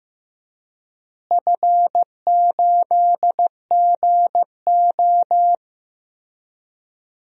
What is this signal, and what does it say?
Morse code, heard as a keyed tone, about 15 wpm, 700 Hz. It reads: F8GO
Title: DXCC-100 Call Signs - 4 Characters - Repeated 3x - 15wpm